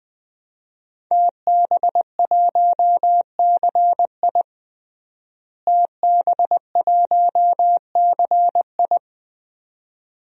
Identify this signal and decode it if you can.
Morse code, heard as a keyed tone, about 21 wpm, 700 Hz. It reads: TB1CI TB1CI
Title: DXCC-100 Call Signs - 5 Characters - Repeated 3x - 20wpm